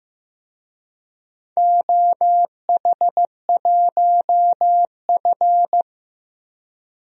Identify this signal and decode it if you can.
Morse code, heard as a keyed tone, about 15 wpm, 700 Hz. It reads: OH1F
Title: DXCC-100 Call Signs - 4 Characters - Repeated 3x - 15wpm